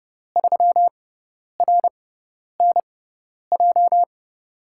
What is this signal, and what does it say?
Morse code, heard as a keyed tone, about 30 wpm, spaced as for 10 wpm, 700 Hz. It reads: 3RNJ
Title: USA Call Signs - 5 Characters - 6x Character Spacing - 30wpm